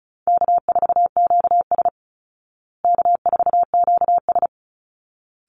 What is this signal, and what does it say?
Morse code, heard as a keyed tone, about 34 wpm, 700 Hz. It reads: K4QS K4QS
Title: Call Signs Repeated 3x - 35wpm